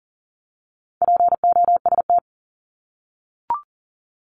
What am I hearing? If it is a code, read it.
Morse code, heard as a keyed tone, about 40 wpm, 700 Hz. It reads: POST
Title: Top 751-1000 Words 40wpm